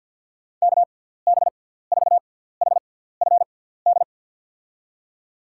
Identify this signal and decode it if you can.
Morse code, heard as a keyed tone, about 49 wpm, spaced as for 17 wpm, 700 Hz. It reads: KB4HFD
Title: USA Call Signs - 6 Characters - 6x Character Spacing - 50wpm